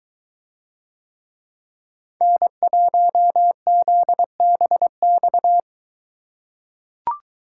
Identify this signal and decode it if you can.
Morse code, heard as a keyed tone, about 22 wpm, 700 Hz. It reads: N1ZBX